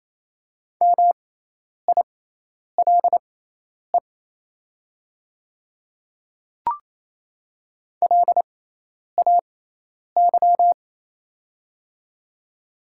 Morse code, heard as a keyed tone, about 28 wpm, spaced as for 10 wpm, 700 Hz. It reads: MILE LAY
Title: 6x Character Spacing - Top 500 Words - 28wpm